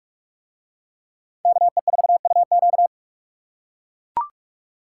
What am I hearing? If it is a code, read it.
Morse code, heard as a keyed tone, about 44 wpm, 700 Hz. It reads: KE4UQ